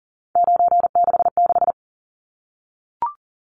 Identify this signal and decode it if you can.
Morse code, heard as a keyed tone, about 40 wpm, 700 Hz. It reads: 966
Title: Numbers - 3-Digit 40wpm